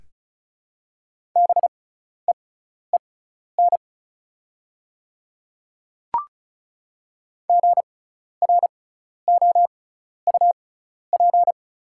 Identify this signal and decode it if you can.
Morse code, heard as a keyed tone, about 36 wpm, spaced as for 12 wpm, 690 Hz. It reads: BEEN GROUP
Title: 6x Character Spacing - Top 300 Words - 35wpm